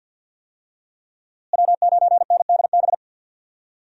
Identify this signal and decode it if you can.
Morse code, heard as a keyed tone, about 51 wpm, 700 Hz. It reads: W9NDB
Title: USA Call Signs - 5 Characters - Repeated 3x - 50wpm